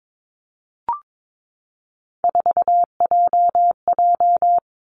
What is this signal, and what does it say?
Morse code, heard as a keyed tone, about 22 wpm, 700 Hz. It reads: 4JJ
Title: Three Challenging Letter-Number Combinations 22wpm